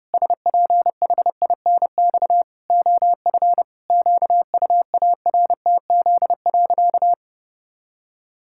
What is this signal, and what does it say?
Morse code, heard as a keyed tone, about 28 wpm, 700 Hz. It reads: SPHINX OF QUARTZ.
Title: Pangram Challenge v2 - 30wpm